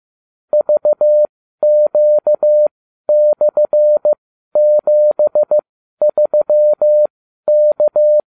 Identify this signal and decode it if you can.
Morse code, heard as a keyed tone, about 15 wpm, spaced as for 12 wpm, 600 Hz. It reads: VQ/73K